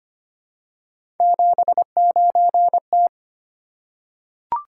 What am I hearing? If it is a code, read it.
Morse code, heard as a keyed tone, about 25 wpm, 700 Hz. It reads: 79T
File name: Three Letter-Number Combinations - 25wpm